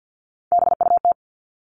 Morse code, heard as a keyed tone, about 50 wpm, 700 Hz. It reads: BUT
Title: Top 300 Words 50wpm